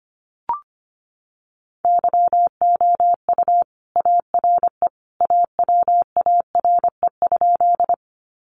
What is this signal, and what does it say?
Morse code, heard as a keyed tone, about 25 wpm, 700 Hz. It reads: YOU ARE AWARE?
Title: Sentences from Top 1200 Words 25wpm